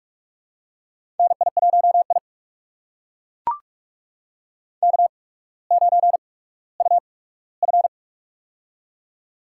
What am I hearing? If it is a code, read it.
Morse code, heard as a keyed tone, about 42 wpm, spaced as for 19 wpm, 700 Hz. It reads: NI1I K9UF
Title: USA Call Signs - 4 Characters - 8x Character Spacing - 45wpm